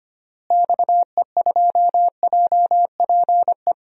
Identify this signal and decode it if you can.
Morse code, heard as a keyed tone, about 25 wpm, 700 Hz. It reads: XE2JPE
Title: DXCC-100 Call Signs - 6 Characters - Repeated 2x - 25wpm